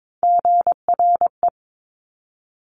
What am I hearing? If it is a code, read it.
Morse code, heard as a keyed tone, about 22 wpm, 700 Hz. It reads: GRE
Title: Top Three Letter Combinations in Words 22wpm